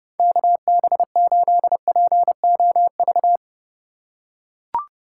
Morse code, heard as a keyed tone, about 30 wpm, 700 Hz. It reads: KB8POV
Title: USA Call Signs - 3x Character Spacing - 30wpm